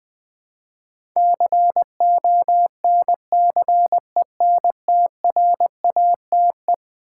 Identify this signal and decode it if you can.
Morse code, heard as a keyed tone, about 20 wpm, 700 Hz. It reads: CONCENTRATE
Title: Eleven Letter Words - 20wpm